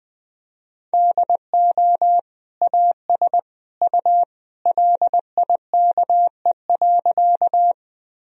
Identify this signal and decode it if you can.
Morse code, heard as a keyed tone, about 20 wpm, 700 Hz. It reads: DO AS U LIKE.
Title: Sentences from Top 100 Words and Abbreviations 20wpm